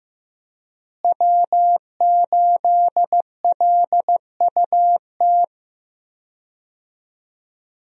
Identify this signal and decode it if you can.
Morse code, heard as a keyed tone, about 15 wpm, 700 Hz. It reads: W8LUT
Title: USA Call Signs - 5 Characters - Repeated 3x - 15wpm